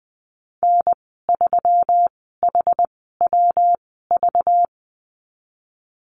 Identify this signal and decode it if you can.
Morse code, heard as a keyed tone, about 20 wpm, spaced as for 15 wpm, 700 Hz. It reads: N3HWV
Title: 193 Lesson 31g - Letter v - Call Signs 20wpm